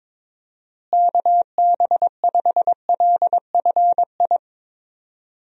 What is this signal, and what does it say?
Morse code, heard as a keyed tone, about 22 wpm, 700 Hz. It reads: KB5LFI
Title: USA Call Signs - 6 Characters - Repeated 3x - 22wpm